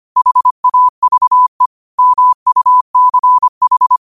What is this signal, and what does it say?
Morse code, heard as a keyed tone, about 25 wpm, 1000 Hz. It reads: HAVE MUCH
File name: qotdfiles / 25wpm